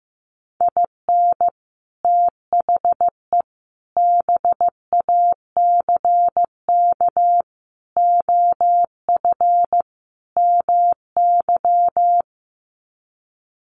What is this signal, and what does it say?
Morse code, heard as a keyed tone, about 14 wpm, 700 Hz. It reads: IN THE BACK OF MY
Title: Sets of 5 Words Encore 15wpm